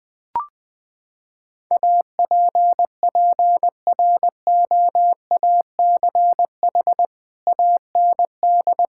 Morse code, heard as a keyed tone, about 20 wpm, 700 Hz. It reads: APPROACH AND
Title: Sentences from Top 1600 Words 20wpm